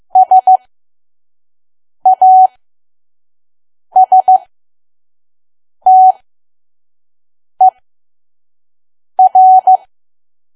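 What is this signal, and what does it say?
Morse code, heard as a keyed tone, about 15 wpm, spaced as for 5 wpm, 750 Hz. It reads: SASTER
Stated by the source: Morse code practice at 5 WPM from KQ2P.COM.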